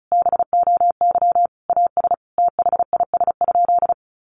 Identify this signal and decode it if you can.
Morse code, heard as a keyed tone, about 34 wpm, 700 Hz. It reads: BOY AS THIS?
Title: Sentences from Top 400 Words 35wpm